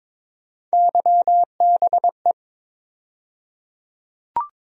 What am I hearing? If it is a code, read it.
Morse code, heard as a keyed tone, about 22 wpm, 700 Hz. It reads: YBE
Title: USA Call Sign Postfixes - 3 Letters 22wpm